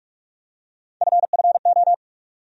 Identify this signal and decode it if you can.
Morse code, heard as a keyed tone, about 44 wpm, 700 Hz. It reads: FFO
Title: USA Call Sign Postfixes - 3 Letters 45wpm